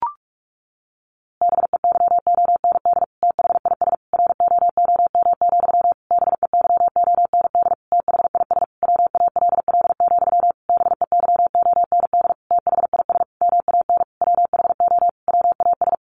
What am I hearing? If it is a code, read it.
Morse code, heard as a keyed tone, about 44 wpm, 700 Hz. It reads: BEYOND THIS ROOM, BEYOND THIS WALL, BEYOND THIS MAN WHO WAS
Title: A Sound Of Thunder 45wpm